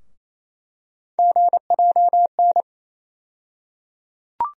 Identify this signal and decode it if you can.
Morse code, heard as a keyed tone, about 28 wpm, 700 Hz. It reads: GJN
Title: USA Call Sign Postfixes - 3 Letters 28wpm